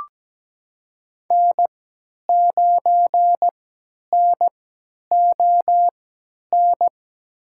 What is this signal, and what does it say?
Morse code, heard as a keyed tone, about 17 wpm, spaced as for 10 wpm, 700 Hz. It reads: N9NON
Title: USA Call Signs - 3x Character Spacing - 17wpm